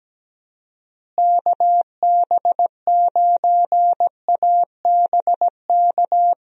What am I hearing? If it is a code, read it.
Morse code, heard as a keyed tone, about 17 wpm, 700 Hz. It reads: KB9ABK
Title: USA Call Signs - 6 Characters - Repeated 3x - 17wpm